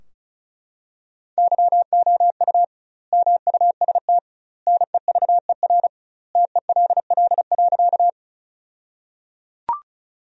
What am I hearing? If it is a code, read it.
Morse code, heard as a keyed tone, about 34 wpm, 700 Hz. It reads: YOU MUST NEVER TELL.
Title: 2x Word-Spacing - Sentences from Top 300 Words - 35wpm